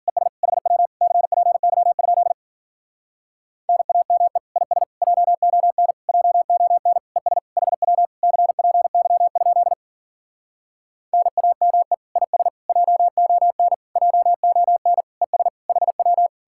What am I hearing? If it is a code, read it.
Morse code, heard as a keyed tone, about 52 wpm, 700 Hz. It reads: ES HW CPY? NAME IS JON JON ES HW CPY? NAME IS JON JON ES HW